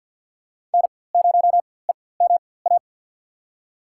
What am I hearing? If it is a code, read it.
Morse code, heard as a keyed tone, about 50 wpm, spaced as for 23 wpm, 700 Hz. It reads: N0EMA
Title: USA Call Signs - 5 Characters - 4x Character Spacing - 50wpm